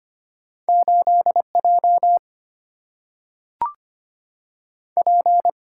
Morse code, heard as a keyed tone, about 25 wpm, 700 Hz. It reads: 8 J P
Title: Single-Letter Single-Number 25wpm